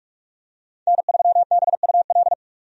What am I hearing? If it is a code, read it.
Morse code, heard as a keyed tone, about 42 wpm, 700 Hz. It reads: N3BUL
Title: USA Call Signs - 5 Characters - 45wpm